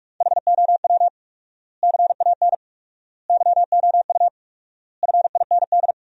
Morse code, heard as a keyed tone, about 42 wpm, 700 Hz. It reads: HOW CAN YOU FIND
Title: 4x Word-Spacing - Sentences from Top 200 Words - 45wpm